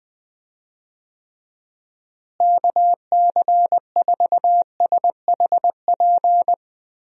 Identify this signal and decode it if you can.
Morse code, heard as a keyed tone, about 20 wpm, 700 Hz. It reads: KC4SHP
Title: USA Call Signs - 6 Characters - Repeated 3x - 20wpm